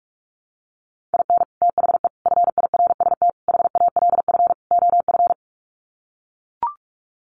Morse code, heard as a keyed tone, about 44 wpm, 700 Hz. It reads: IN THE FIRST HALF OF
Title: Sets of 5 Words 45wpm